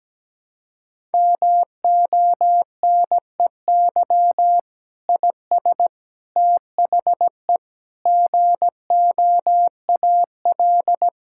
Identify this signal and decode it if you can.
Morse code, heard as a keyed tone, about 17 wpm, 700 Hz. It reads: MONEY IS THE GOAL
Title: Sentences from Top 1000 Words - Review Part 5 - 17wpm